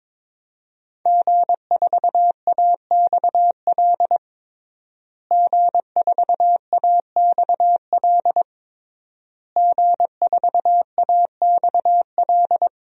Morse code, heard as a keyed tone, about 22 wpm, 700 Hz. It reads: G4AXL G4AXL G4AXL
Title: DXCC-100 Call Signs - 5 Characters - Repeated 3x - 22wpm